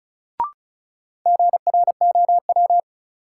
Morse code, heard as a keyed tone, about 35 wpm, 700 Hz. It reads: GROW
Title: Sentences from Top 300 Words - New Word Drill - 35wpm